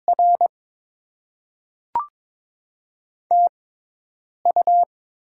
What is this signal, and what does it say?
Morse code, heard as a keyed tone, about 22 wpm, spaced as for 8 wpm, 700 Hz. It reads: R TU
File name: Top 20 QSO Elements - 6x Character Spacing - 22wpm